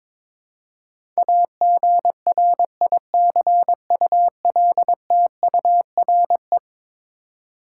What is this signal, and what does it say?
Morse code, heard as a keyed tone, about 22 wpm, 700 Hz. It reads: AGRICULTURE
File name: practice / 22wpm / Eleven Letter Words - 22wpm